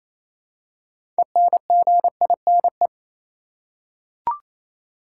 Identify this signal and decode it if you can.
Morse code, heard as a keyed tone, about 28 wpm, 700 Hz. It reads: ENGINE
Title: Six Letter Words - 28wpm